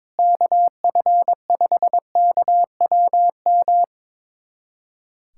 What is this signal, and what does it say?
Morse code, heard as a keyed tone, about 22 wpm, 700 Hz. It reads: KF5KWM
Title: USA Callsign DE Callsign 22wpm